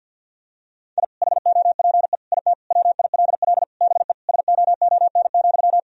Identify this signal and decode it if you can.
Morse code, heard as a keyed tone, about 52 wpm, 700 Hz. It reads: I HOPE IT WILL BE SOON,
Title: Sentences from Top 1000 Words - Review Part 2 - 50wpm